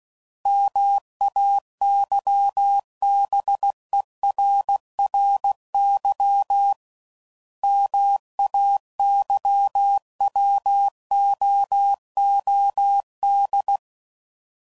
Morse code, heard as a keyed tone, about 16 wpm, 800 Hz. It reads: MAYBERRY MAYWOOD
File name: CW Code Groups @15WPM / fromSM's / Cities